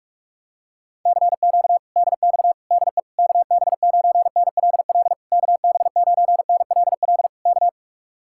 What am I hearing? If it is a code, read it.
Morse code, heard as a keyed tone, about 49 wpm, 700 Hz. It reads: CQ DX DE KB9NLL KB9NLL K